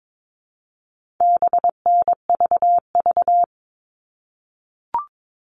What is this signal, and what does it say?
Morse code, heard as a keyed tone, about 22 wpm, 700 Hz. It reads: BNVV
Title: All Cut Numbers - 4-Digit 22wpm